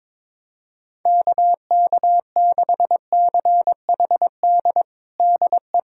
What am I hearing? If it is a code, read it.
Morse code, heard as a keyed tone, about 21 wpm, 700 Hz. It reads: KK6CHD DE
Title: USA Callsign DE Callsign - 6 Characters - 22wpm